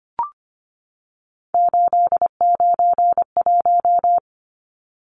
Morse code, heard as a keyed tone, about 25 wpm, 690 Hz. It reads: 891
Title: Numbers - 3-Digit 25wpm